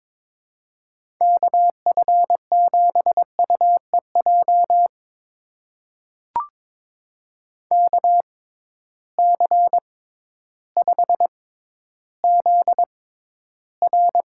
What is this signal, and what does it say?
Morse code, heard as a keyed tone, about 22 wpm, spaced as for 12 wpm, 700 Hz. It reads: KF7UEJ KC5ZR
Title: USA Call Signs - 6x Character Spacing - 22wpm